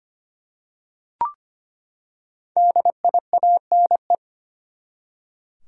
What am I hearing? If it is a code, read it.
Morse code, heard as a keyed tone, about 25 wpm, 700 Hz. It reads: DIANE
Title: USA Common Names - 25wpm